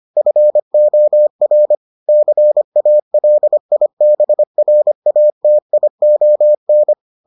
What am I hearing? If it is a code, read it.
Morse code, heard as a keyed tone, about 25 wpm, 600 Hz. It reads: FOR CALIBRATION